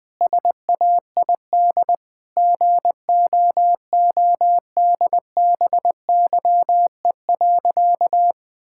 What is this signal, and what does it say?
Morse code, heard as a keyed tone, about 21 wpm, 700 Hz. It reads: SAID GOODBYE.
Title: Sentences with 4-Word Phrases - Spoken Beginning - 20wpm